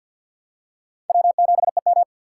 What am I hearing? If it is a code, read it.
Morse code, heard as a keyed tone, about 51 wpm, 690 Hz. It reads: W7EM
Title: USA Call Signs - 2x Character Spacing - 50wpm